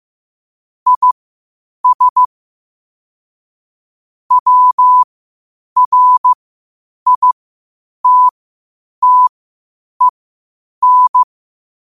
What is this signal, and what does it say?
Morse code, heard as a keyed tone, about 15 wpm, spaced as for 8 wpm, 1000 Hz. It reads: IS WRITTEN
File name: qotdfiles / 8wpm